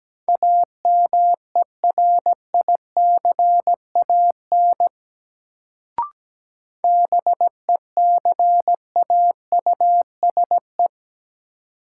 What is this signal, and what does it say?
Morse code, heard as a keyed tone, about 17 wpm, 700 Hz. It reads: AMERICAN BECAUSE
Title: Sets of 2 Words - New Word Drill - 17wpm